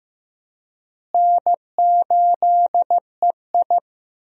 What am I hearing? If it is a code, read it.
Morse code, heard as a keyed tone, about 15 wpm, 700 Hz. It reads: N8EI
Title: USA Call Signs - 4 Characters - Repeated 2x - 15wpm